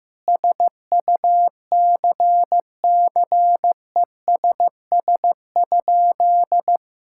Morse code, heard as a keyed tone, about 15 wpm, 700 Hz. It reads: SUCCESS?
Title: Sentences from Top 1000 Words - Review Part 6 - 15wpm